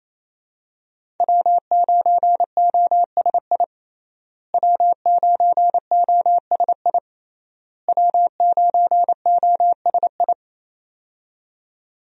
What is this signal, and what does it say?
Morse code, heard as a keyed tone, about 28 wpm, 700 Hz. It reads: W9OSI W9OSI W9OSI
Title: USA Call Signs - 5 Characters - Repeated 3x - 28wpm